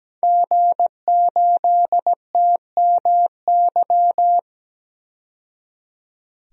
Morse code, heard as a keyed tone, about 17 wpm, 700 Hz. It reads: G8TMY
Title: DXCC-100 Call Signs - 5 Characters - 17wpm